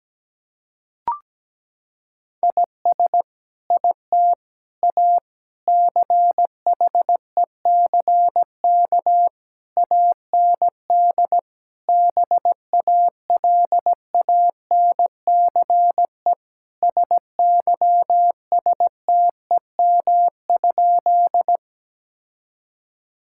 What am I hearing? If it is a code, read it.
Morse code, heard as a keyed tone, about 17 wpm, 700 Hz. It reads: IS IT A CHECK AND BALANCE SYSTEM?
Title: Sentences from Top 1000 Words - Review Part 6 - 17wpm